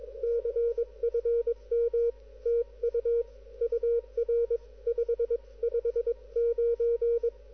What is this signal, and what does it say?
Morse code, heard as a keyed tone, about 21 wpm, 440 Hz, with radio noise. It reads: CFM TU UR 559
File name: pa0oos / recall / hl5ivl 160